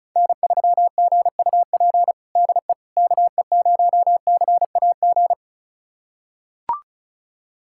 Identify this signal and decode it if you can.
Morse code, heard as a keyed tone, about 36 wpm, 700 Hz. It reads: N3GUP DE KE0CAG
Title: USA Callsign DE Callsign 35wpm